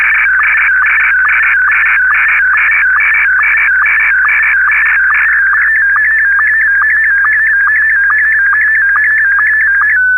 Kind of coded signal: SSTV (slow-scan television)